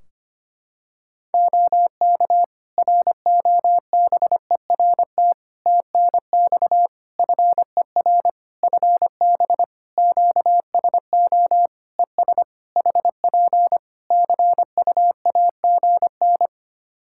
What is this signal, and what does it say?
Morse code, heard as a keyed tone, about 25 wpm, 700 Hz. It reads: OK ROBERT TNX FER FB QSO ES HP CUAGN
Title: Practice QSO - Exchange 4 - Ending - 25wpm